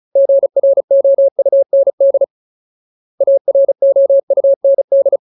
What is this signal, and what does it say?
Morse code, heard as a keyed tone, about 34 wpm, 550 Hz. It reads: GROUND AROUND
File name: Ultimate Word List / 35WPM Ultimate